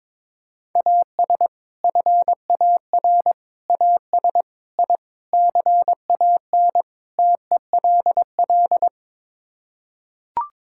Morse code, heard as a keyed tone, about 21 wpm, 700 Hz. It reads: AS FAR AS I CAN TELL
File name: practice / 22wpm / Sets of 6 Words - 22wpm